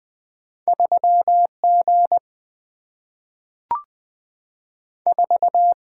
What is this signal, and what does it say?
Morse code, heard as a keyed tone, about 20 wpm, 700 Hz. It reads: 3G 4
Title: Single-Letter Single-Number 20wpm